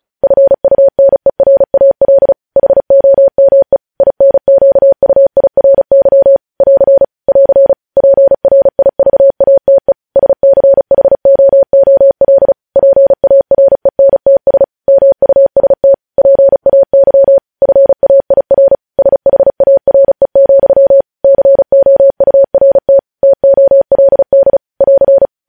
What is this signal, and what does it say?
Morse code, heard as a keyed tone, about 36 wpm, 550 Hz. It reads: FUNERAL HOME INQUIRY <AR> <AR> PRIVATE SCHOOL PARENTS MUST PAY FAIR SHARE, COURT TOLD <AR>
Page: news headlines in morse code at 35 wpm